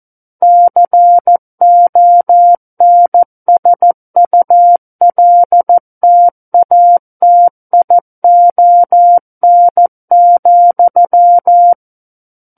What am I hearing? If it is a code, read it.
Morse code, heard as a keyed tone, about 14 wpm, 700 Hz. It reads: CONSULTATION,
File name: Book One / 05-Chapter 5 - 14 WPM